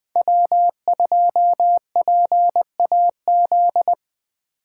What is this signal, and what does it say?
Morse code, heard as a keyed tone, about 20 wpm, 700 Hz. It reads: W2PAZ